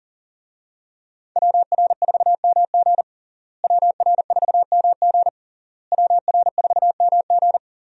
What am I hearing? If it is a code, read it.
Morse code, heard as a keyed tone, about 40 wpm, 700 Hz. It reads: WR4MG WR4MG WR4MG
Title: USA Call Signs - 5 Characters - Repeated 3x - 40wpm